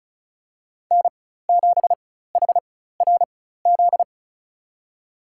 Morse code, heard as a keyed tone, about 36 wpm, spaced as for 16 wpm, 700 Hz. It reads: N7HRZ
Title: USA Call Signs - 4x Character Spacing - 35wpm